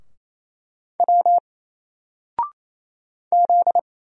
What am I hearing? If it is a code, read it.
Morse code, heard as a keyed tone, about 28 wpm, 700 Hz. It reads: W Z
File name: Single-Letter-Number-28wpm